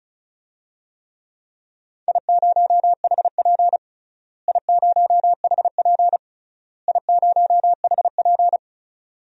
Morse code, heard as a keyed tone, about 34 wpm, 700 Hz. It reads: I0HP I0HP I0HP
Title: DXCC-100 Call Signs - 4 Characters - Repeated 3x - 35wpm